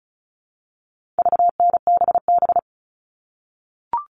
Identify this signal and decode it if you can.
Morse code, heard as a keyed tone, about 34 wpm, 700 Hz. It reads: VNBB
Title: All Cut Numbers - 4-Digit 35wpm